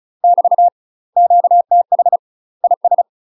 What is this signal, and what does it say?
Morse code, heard as a keyed tone, about 36 wpm, 700 Hz. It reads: <BT> QTH IS
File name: UK MASTER QSOs TRAINING / 35WPM / UK MASTER QSOs 0035